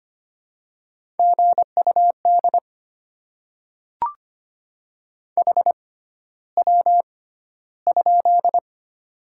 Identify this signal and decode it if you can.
Morse code, heard as a keyed tone, about 25 wpm, spaced as for 13 wpm, 700 Hz. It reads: GUD HW?
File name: Top 20 QSO Elements - 6x Character Spacing - 25wpm